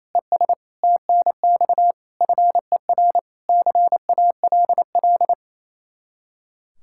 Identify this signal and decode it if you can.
Morse code, heard as a keyed tone, about 34 wpm, 700 Hz. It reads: ES TNX FER CALL
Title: Speed-Racing Common QSO Phrases - 28wpm